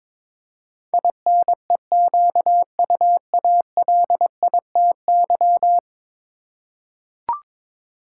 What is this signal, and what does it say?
Morse code, heard as a keyed tone, about 22 wpm, 700 Hz. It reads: INEQUALITY
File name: practice / 22wpm / 10 Letter Words - First 3 Letters Given - 22wpm